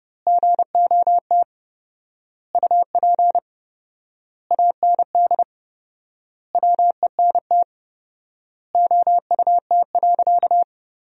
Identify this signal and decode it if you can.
Morse code, heard as a keyed tone, about 30 wpm, 700 Hz. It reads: GOT UP AND WENT OUT.
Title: 4x Word-Spacing - Sentences from Top 300 Words - 30wpm